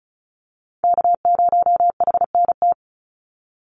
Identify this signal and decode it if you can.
Morse code, heard as a keyed tone, about 34 wpm, 700 Hz. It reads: K0HNT
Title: USA Call Signs 35wpm